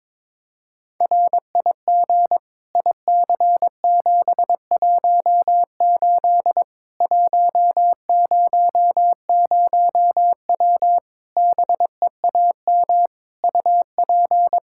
Morse code, heard as a keyed tone, about 22 wpm, 700 Hz. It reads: RIG IC718 100W BEAM UP
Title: K1USN SST QSO Practice 22wpm